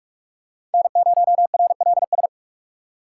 Morse code, heard as a keyed tone, about 44 wpm, 700 Hz. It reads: N0RLS